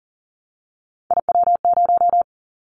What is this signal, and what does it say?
Morse code, heard as a keyed tone, about 40 wpm, 700 Hz. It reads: IW0
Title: Callsign Prefixes for Top 25 DXCC - Repeated 3x - 40wpm